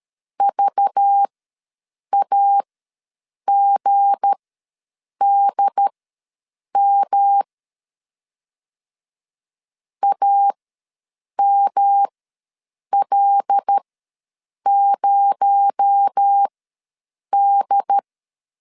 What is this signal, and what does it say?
Morse code, heard as a keyed tone, about 13 wpm, spaced as for 7 wpm, 800 Hz. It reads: VAGDM AML0D
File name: LEZIONI / 7 GF9Z / 7 G ws 12wpm 3140